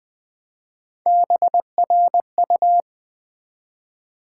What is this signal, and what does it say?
Morse code, heard as a keyed tone, about 20 wpm, 700 Hz. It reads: BRU
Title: Build A Word - 1 Letter At A Time - 5-Letter Words - 20wpm